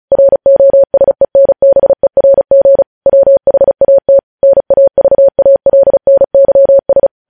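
Morse code, heard as a keyed tone, about 36 wpm, 550 Hz. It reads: ROSENBERG WHAT NAVALNYS
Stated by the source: Morse practice at 35 wpm sending news headlines.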